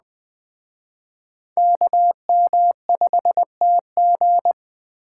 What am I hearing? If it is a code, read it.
Morse code, heard as a keyed tone, about 20 wpm, 700 Hz. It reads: KM5TG
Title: Speed Racing - USA 2x2 Call Signs 17wpm